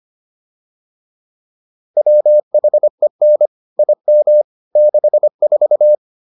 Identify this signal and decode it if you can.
Morse code, heard as a keyed tone, about 25 wpm, 600 Hz. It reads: WHEN IM 64